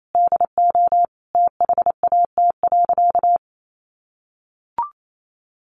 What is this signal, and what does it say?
Morse code, heard as a keyed tone, about 28 wpm, 700 Hz. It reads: DO THAT.
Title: Sentences from Top 1500 Words 28wpm